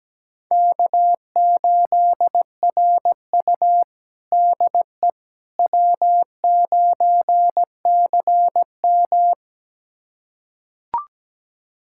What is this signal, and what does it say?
Morse code, heard as a keyed tone, about 17 wpm, 700 Hz. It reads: K8RU DE W9CM
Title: USA Callsign DE Callsign - 4 Characters - 17wpm